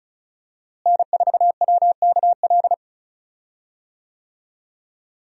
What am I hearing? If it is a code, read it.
Morse code, heard as a keyed tone, about 34 wpm, 700 Hz. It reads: N4WKL
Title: USA Call Signs - 5 Characters - Repeated 3x - 35wpm